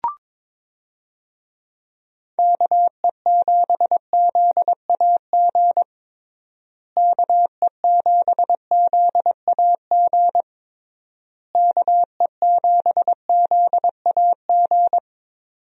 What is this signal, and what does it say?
Morse code, heard as a keyed tone, about 22 wpm, 700 Hz. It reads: KE7ZAG KE7ZAG KE7ZAG
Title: USA Call Signs - 6 Characters - Repeated 3x - 22wpm